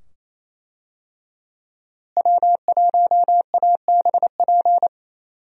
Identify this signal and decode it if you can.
Morse code, heard as a keyed tone, about 28 wpm, 700 Hz. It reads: W1ABP
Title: USA Call Signs - 5 Characters - Repeated 2x - 28wpm